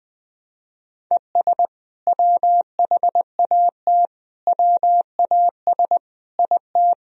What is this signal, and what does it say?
Morse code, heard as a keyed tone, about 20 wpm, 700 Hz. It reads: ES WHAT WAS IT
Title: Sentences from Top 100 Words and Abbreviations 20wpm